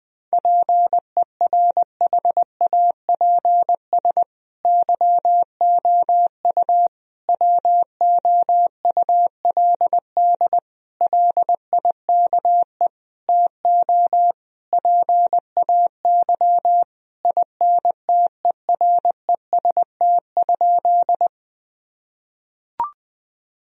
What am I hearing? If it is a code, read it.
Morse code, heard as a keyed tone, about 20 wpm, 700 Hz. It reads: PERHAPS YOU WOULD LIKE TO PAY INTEREST?
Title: Sentences from Top 1000 Words - Review Part 2 - 20wpm